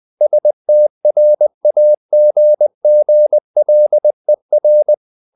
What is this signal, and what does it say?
Morse code, heard as a keyed tone, about 20 wpm, 600 Hz. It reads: STRAGGLER
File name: the old man and the sea / the old man and the sea part 1 20wpm